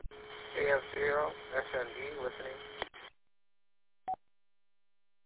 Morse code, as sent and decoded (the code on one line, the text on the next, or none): none